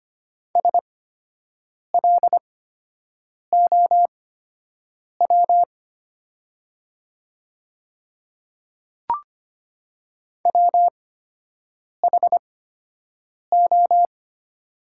Morse code, heard as a keyed tone, about 25 wpm, spaced as for 7 wpm, 700 Hz. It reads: SLOW WHO
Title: 8x Character Spacing - Top 500 Words - 25wpm